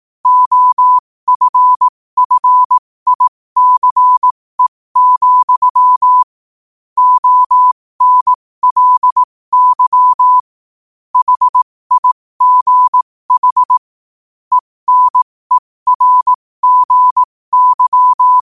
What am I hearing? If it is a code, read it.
Morse code, heard as a keyed tone, about 18 wpm, spaced as for 15 wpm, 1000 Hz. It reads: OFFICE, ONLY HIGH ENERGY
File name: qotdfiles / 15wpm